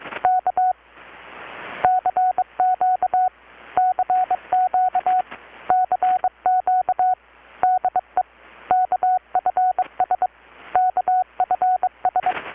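Morse code, heard as a keyed tone, about 22 wpm, 720 Hz. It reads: K CQ CQ CQ DE KFS KFS
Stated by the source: Morse from KPH's Night of Nights XX maritime radiotelegraphy event, received in Chiapas, Mexico about 3,800 km away on a Kenwood R-600 with a grounded 10 m long-wire antenna.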